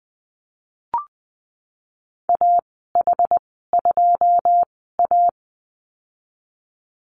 Morse code, heard as a keyed tone, about 20 wpm, spaced as for 15 wpm, 700 Hz. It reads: AH2A
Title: 088 Lesson 15g - Number 5 - Call Signs 20wpm